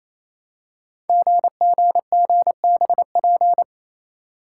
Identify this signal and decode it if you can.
Morse code, heard as a keyed tone, about 29 wpm, 700 Hz. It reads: GGGBP
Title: Code Groups - 5 Letters - 28wpm